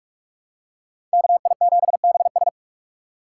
Morse code, heard as a keyed tone, about 47 wpm, 700 Hz. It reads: KI7BS
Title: USA Call Signs - 5 Characters - 4x Character Spacing - 45wpm